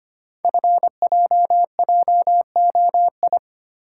Morse code, heard as a keyed tone, about 25 wpm, 700 Hz. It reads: FJJOI
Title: Code Groups - 5 Letters - 25wpm